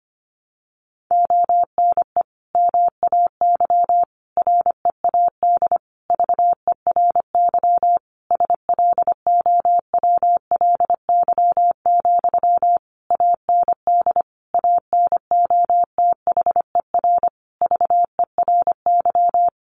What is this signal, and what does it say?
Morse code, heard as a keyed tone, about 25 wpm, 700 Hz. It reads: ONE MAY READ VERY SLOWLY, AND ANOTHER VERY
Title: Sentences from Top 1100 Words 25wpm